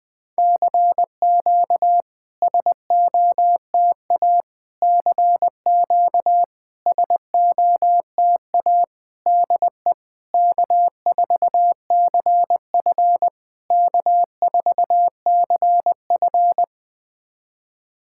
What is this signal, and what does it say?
Morse code, heard as a keyed tone, about 20 wpm, 700 Hz. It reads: CQ SOTA CQ SOTA DE K4CF K4CF
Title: SOTA QSO Practice 20wpm